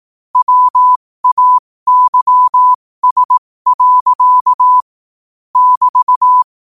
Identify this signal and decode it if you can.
Morse code, heard as a keyed tone, about 18 wpm, spaced as for 15 wpm, 1000 Hz. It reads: WAYS. <BT>